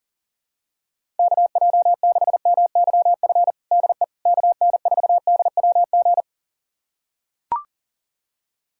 Morse code, heard as a keyed tone, about 40 wpm, 700 Hz. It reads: KJ6MYF DE KN4DWG
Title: USA Callsign DE Callsign 40wpm